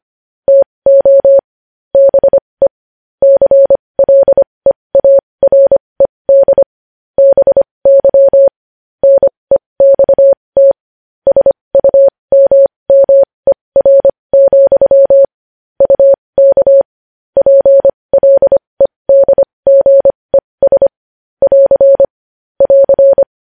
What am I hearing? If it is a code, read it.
Morse code, heard as a keyed tone, about 25 wpm, spaced as for 20 wpm, 550 Hz. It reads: TO BE CLEARED BY NEXT SUMMER, UK PLEDGES <AR> <AR>
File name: podcast / News headlines-1x-25x20